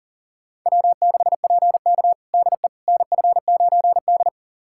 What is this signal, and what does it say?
Morse code, heard as a keyed tone, about 40 wpm, 700 Hz. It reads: W6PK DE NF9D
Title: USA Callsign DE Callsign - 4 Characters - 40wpm